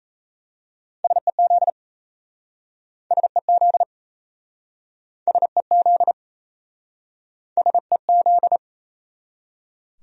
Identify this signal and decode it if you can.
Morse code, heard as a keyed tone, about 38 wpm, 700 Hz. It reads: SEZ SEZ SEZ SEZ